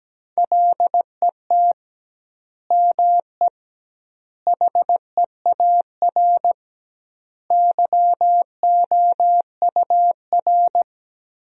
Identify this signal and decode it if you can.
Morse code, heard as a keyed tone, about 17 wpm, 700 Hz. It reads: LET ME HEAR YOUR
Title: 2x Word-Spacing - Sentences from Top 300 Words - 17wpm